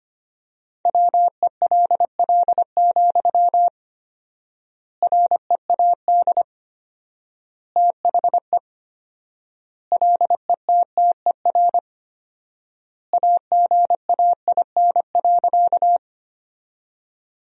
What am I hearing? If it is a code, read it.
Morse code, heard as a keyed tone, about 25 wpm, 700 Hz. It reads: WELL, READ THE LETTER AGAIN.
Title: 4x Word-Spacing - Sentences from Top 300 Words - 25wpm